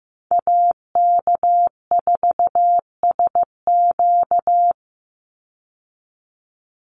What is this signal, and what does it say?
Morse code, heard as a keyed tone, about 15 wpm, 700 Hz. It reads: AK4SQ
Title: Call Signs Repeated 3x - 15wpm